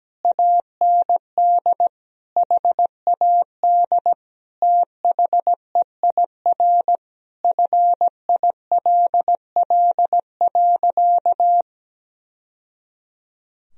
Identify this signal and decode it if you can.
Morse code, heard as a keyed tone, about 17 wpm, 700 Hz. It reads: AND HAD THEIR FILL.
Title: Sentences from Top 1000 Words - Review Part 3 - 17wpm